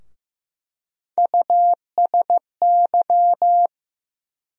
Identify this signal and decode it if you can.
Morse code, heard as a keyed tone, about 15 wpm, 700 Hz. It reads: USY
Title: USA Call Sign Postfixes - 3 Letters 15wpm